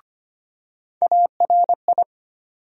Morse code, heard as a keyed tone, about 25 wpm, 700 Hz. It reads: ARI
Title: USA Call Sign Postfixes - 3 Letters 25wpm